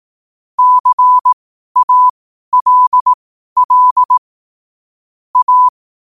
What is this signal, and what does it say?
Morse code, heard as a keyed tone, about 18 wpm, spaced as for 12 wpm, 1000 Hz. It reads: CALL A
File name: qotdfiles / 12wpm